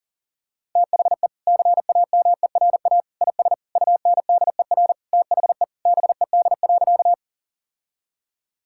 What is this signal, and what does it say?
Morse code, heard as a keyed tone, about 40 wpm, 700 Hz. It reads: THE CAMERA IS UNDER THE BED.
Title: Sentences from Top 1000 Words - Review Part 5 - 40wpm